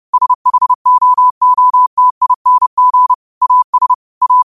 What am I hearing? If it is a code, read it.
Morse code, heard as a keyed tone, about 30 wpm, 1000 Hz. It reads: SHOOTING AS A